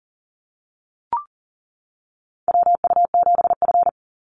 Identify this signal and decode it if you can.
Morse code, heard as a keyed tone, about 40 wpm, 700 Hz. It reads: WU7F
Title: Call Signs - 40wpm